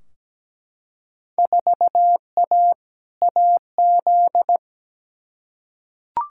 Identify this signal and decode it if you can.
Morse code, heard as a keyed tone, about 17 wpm, 700 Hz. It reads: 4A AZ
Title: Field Day Exchanges - 17wpm